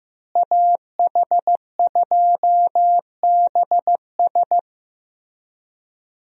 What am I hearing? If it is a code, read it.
Morse code, heard as a keyed tone, about 16 wpm, 700 Hz. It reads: AH2BS
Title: USA Call Signs - 5 Characters - Repeated 3x - 15wpm